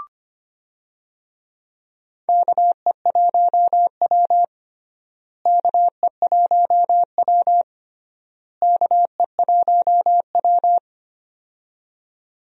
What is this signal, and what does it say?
Morse code, heard as a keyed tone, about 25 wpm, 700 Hz. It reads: KE1W KE1W KE1W
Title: USA Call Signs - 4 Characters - Repeated 3x - 25wpm